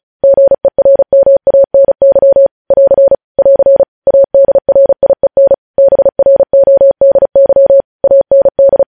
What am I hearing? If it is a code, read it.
Morse code, heard as a keyed tone, about 36 wpm, 550 Hz. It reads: GERMANY <AR> <AR> ADRIEN BRODY AND
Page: News Headlines in Morse Code at 35 WPM